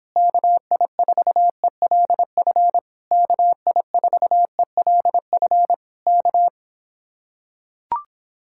Morse code, heard as a keyed tone, about 26 wpm, 700 Hz. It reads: KI4ELF KI4ELF K